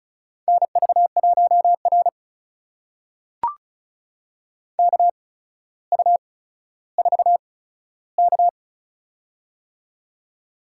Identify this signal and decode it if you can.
Morse code, heard as a keyed tone, about 36 wpm, spaced as for 15 wpm, 700 Hz. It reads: NV1R KU4K
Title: USA Call Signs - 4 Characters - 8x Character Spacing - 35wpm